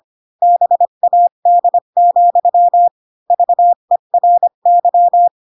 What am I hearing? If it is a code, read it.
Morse code, heard as a keyed tone, about 25 wpm, 700 Hz. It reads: BAD, VERY
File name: Pinocchio / Pinocchio ch7 20 25 ik1hge